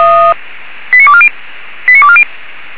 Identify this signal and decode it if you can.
Five-tone selective call (ZVEI), heard as quick runs of five tones, tones of about 70 ms. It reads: 89240 89240
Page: /464/SDR/ZVEI/